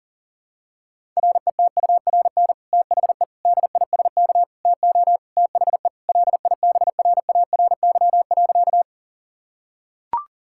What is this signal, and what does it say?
Morse code, heard as a keyed tone, about 37 wpm, 700 Hz. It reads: RETURN THE DISK TO THE LIBRARY.
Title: Sentences from Top 1000 Words - Review Part 6 - 40wpm